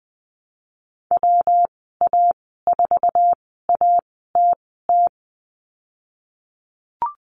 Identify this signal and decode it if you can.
Morse code, heard as a keyed tone, about 20 wpm, spaced as for 15 wpm, 700 Hz. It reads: WA4ATT
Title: 128 Lesson 21g - Number 6 - Call Signs 20wpm